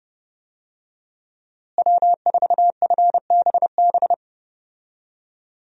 Morse code, heard as a keyed tone, about 32 wpm, 700 Hz. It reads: W4FBB
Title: USA Call Signs - 5 Characters - Repeated 2x - 30wpm